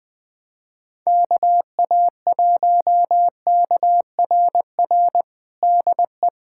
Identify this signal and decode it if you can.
Morse code, heard as a keyed tone, about 20 wpm, 700 Hz. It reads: KA1KRR DE
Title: USA Callsign DE Callsign - 6 Characters - 20wpm